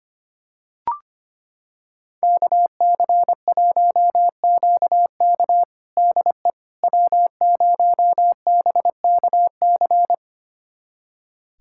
Morse code, heard as a keyed tone, about 25 wpm, 700 Hz. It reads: KC1QK DE W0BKC
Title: USA Callsign DE Callsign - 5 Characters - 25wpm